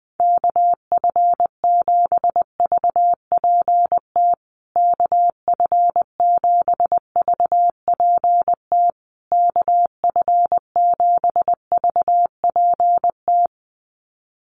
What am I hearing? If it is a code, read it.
Morse code, heard as a keyed tone, about 20 wpm, 700 Hz. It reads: KF7VPT KF7VPT KF7VPT
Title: USA Call Signs Repeated 3x 20wpm